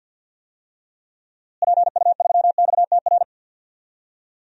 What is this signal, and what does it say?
Morse code, heard as a keyed tone, about 52 wpm, 700 Hz. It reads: PU3XTR
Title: DXCC-100 Call Signs - 6 Characters - Repeated 2x - 50wpm